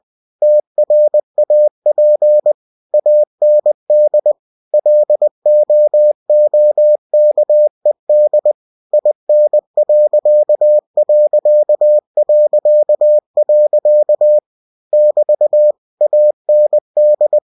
Morse code, heard as a keyed tone, about 20 wpm, 600 Hz. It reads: TRAP AND LOOKED IN.... <BT> AND
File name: cw-books / winnie-the-pooh / winnie-20wpm-page-0042